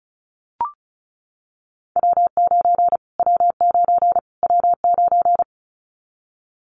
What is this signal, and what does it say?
Morse code, heard as a keyed tone, about 34 wpm, 700 Hz. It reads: W9 W9 W9
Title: Callsign Prefixes for Top 25 DXCC - Repeated 3x - 35wpm